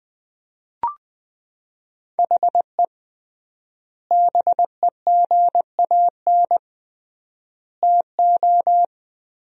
Morse code, heard as a keyed tone, about 20 wpm, 700 Hz. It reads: HE BEGAN TO
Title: 3x Word-Spacing - Sentences from Top 500 Words - 20wpm